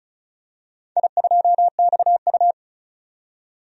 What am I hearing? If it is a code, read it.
Morse code, heard as a keyed tone, about 34 wpm, 700 Hz. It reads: I2XU
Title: DXCC-100 Call Signs - 4 Characters - 35wpm